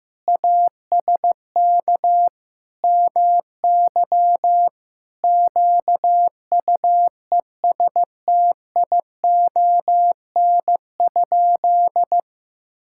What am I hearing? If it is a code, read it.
Morse code, heard as a keyed tone, about 16 wpm, 700 Hz. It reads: ASK MY QUESTION?
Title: Sentences from Top 1000 Words - Review Part 5 - 15wpm